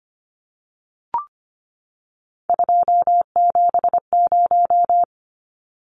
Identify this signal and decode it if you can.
Morse code, heard as a keyed tone, about 25 wpm, 700 Hz. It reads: 270
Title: Numbers - 3-Digit 25wpm